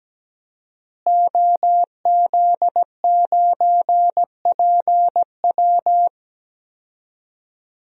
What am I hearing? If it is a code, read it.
Morse code, heard as a keyed tone, about 17 wpm, 700 Hz. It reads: OZ9PW
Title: DXCC-100 Call Signs - 5 Characters - Repeated 2x - 17wpm